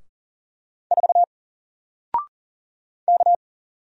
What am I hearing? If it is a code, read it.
Morse code, heard as a keyed tone, about 40 wpm, 700 Hz. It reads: 4 K
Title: Single Letter-Number Alt 40wpm